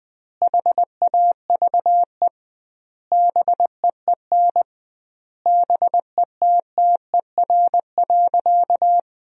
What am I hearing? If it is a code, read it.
Morse code, heard as a keyed tone, about 20 wpm, 700 Hz. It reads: HAVE BEEN BETTER.
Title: 2x Word-Spacing - Sentences from Top 500 Words - 20wpm